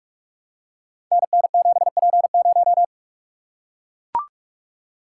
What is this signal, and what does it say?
Morse code, heard as a keyed tone, about 44 wpm, 700 Hz. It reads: NN7P0
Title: Code Groups - 5 Letter:Numbers - 45wpm